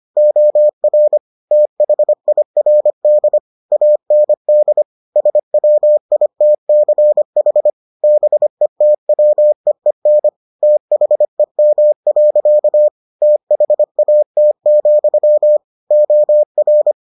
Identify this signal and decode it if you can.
Morse code, heard as a keyed tone, about 25 wpm, 600 Hz. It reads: OR THIRD AND SWITCH BETWEEN THEM. THAT, OR